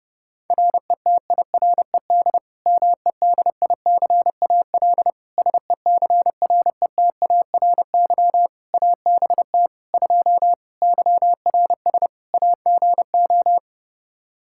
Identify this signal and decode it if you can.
Morse code, heard as a keyed tone, about 32 wpm, 700 Hz. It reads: RETIRED MEDICAL SECRETARY ABT 2 YRS AGO
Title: Practice QSO - Exchange 3 - More Info - 30wpm